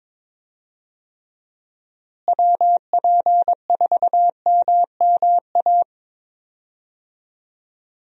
Morse code, heard as a keyed tone, about 22 wpm, 700 Hz. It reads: WP4MMA
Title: USA Call Signs - 6 Characters - Repeated 2x - 22wpm